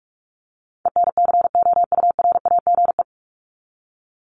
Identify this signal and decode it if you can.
Morse code, heard as a keyed tone, about 44 wpm, 700 Hz. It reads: ENCOURAGE
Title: Top 1251-1500 Words 45wpm